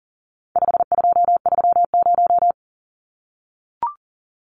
Morse code, heard as a keyed tone, about 40 wpm, 700 Hz. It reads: 5230
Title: Numbers - 4-Digit 40wpm